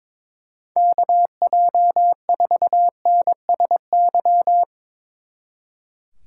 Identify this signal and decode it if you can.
Morse code, heard as a keyed tone, about 24 wpm, 700 Hz. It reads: KJ4NSY
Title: Speed Racing - USA Call Signs 22wpm